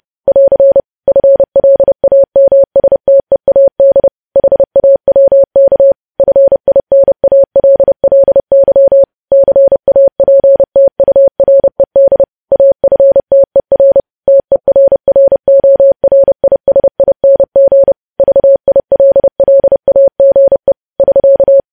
Morse code, heard as a keyed tone, about 30 wpm, 550 Hz. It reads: <AR> FLAMSTEAD HAWK FINALLY CAPTURED AFTER TERRORISING VILLAGE <SK>
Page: News Headlines in Morse Code at 30 WPM